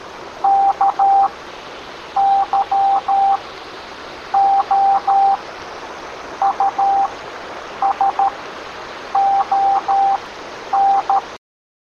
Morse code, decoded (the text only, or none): K YOU SON